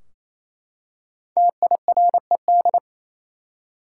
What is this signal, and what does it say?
Morse code, heard as a keyed tone, about 28 wpm, 700 Hz. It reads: TIRED
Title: Five Letter Words - 28wpm